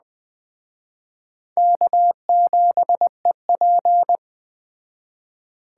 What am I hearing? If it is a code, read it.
Morse code, heard as a keyed tone, about 20 wpm, 700 Hz. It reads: K7EP
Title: Speed Racing - USA Call Signs 17wpm